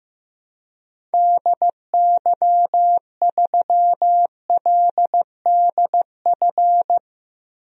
Morse code, heard as a keyed tone, about 15 wpm, 700 Hz. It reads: DY3LDF
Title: DXCC-100 Call Signs - 6 Characters - 15wpm